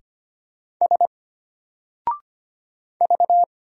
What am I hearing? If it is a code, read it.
Morse code, heard as a keyed tone, about 25 wpm, 700 Hz. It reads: S V
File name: Single-Letter-Number-25wpm